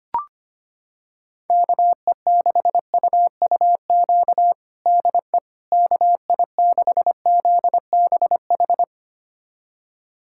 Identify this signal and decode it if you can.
Morse code, heard as a keyed tone, about 25 wpm, 700 Hz. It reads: KE6UUQ DE KI6ZBH
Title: USA Callsign DE Callsign 25wpm